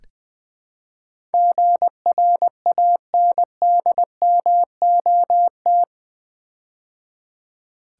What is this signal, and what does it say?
Morse code, heard as a keyed tone, about 20 wpm, 700 Hz. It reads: GRANDMOT
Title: Build A Word - 4 Letters At A Time - 11-Letter Words - 20wpm